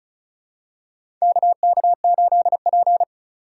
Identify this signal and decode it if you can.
Morse code, heard as a keyed tone, about 34 wpm, 700 Hz. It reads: KK8P
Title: USA Call Signs - 4 Characters - 2x Character Spacing - 35wpm